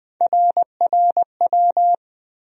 Morse code, heard as a keyed tone, about 21 wpm, 700 Hz. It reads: RRW
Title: USA Call Sign Postfixes - 3 Letters 20wpm